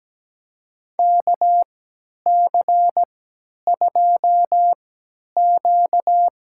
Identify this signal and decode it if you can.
Morse code, heard as a keyed tone, about 17 wpm, spaced as for 10 wpm, 700 Hz. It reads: KC2Q
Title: USA Call Signs - 3x Character Spacing - 17wpm